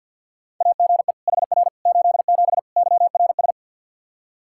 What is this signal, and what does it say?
Morse code, heard as a keyed tone, about 51 wpm, 700 Hz. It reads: AGE HR 87 YRS
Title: Practice QSO - Exchange 3 - More Info - 50wpm